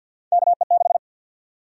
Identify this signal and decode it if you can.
Morse code, heard as a keyed tone, about 51 wpm, 700 Hz. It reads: KE6
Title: USA Call Sign Prefixes - 50wpm